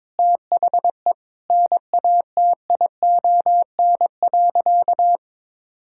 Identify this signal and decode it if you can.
Morse code, heard as a keyed tone, about 22 wpm, 700 Hz. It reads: THE NATION.
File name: practice / 22wpm / Sentences with 8-Word Phrases - 22wpm